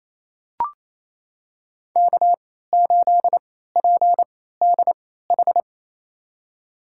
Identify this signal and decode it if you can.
Morse code, heard as a keyed tone, about 28 wpm, spaced as for 16 wpm, 700 Hz. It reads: K8PDH
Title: USA Call Signs - 3x Character Spacing - 28wpm